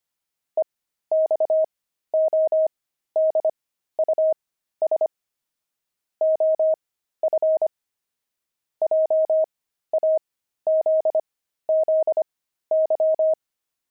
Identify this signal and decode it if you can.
Morse code, heard as a keyed tone, about 25 wpm, spaced as for 13 wpm, 630 Hz. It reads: EXODUS OF JAZZY
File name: pangram-fs-13-cs-25-pigeons